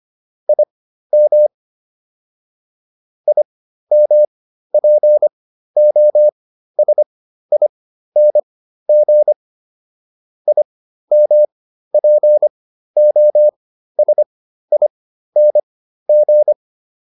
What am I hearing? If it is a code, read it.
Morse code, heard as a keyed tone, about 25 wpm, spaced as for 13 wpm, 600 Hz. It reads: IM IMPOSING IMPOSING